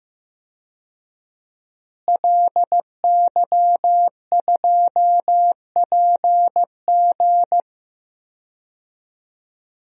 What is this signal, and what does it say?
Morse code, heard as a keyed tone, about 15 wpm, 700 Hz. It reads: LY2PG
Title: DXCC-100 Call Signs - 5 Characters - Repeated 2x - 15wpm